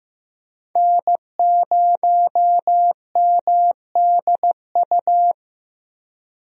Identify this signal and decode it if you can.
Morse code, heard as a keyed tone, about 15 wpm, 700 Hz. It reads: N0MDU
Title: USA Call Signs - 5 Characters - Repeated 3x - 15wpm